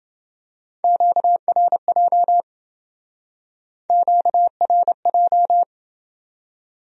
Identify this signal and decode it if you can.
Morse code, heard as a keyed tone, about 27 wpm, 700 Hz. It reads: QRJ QRJ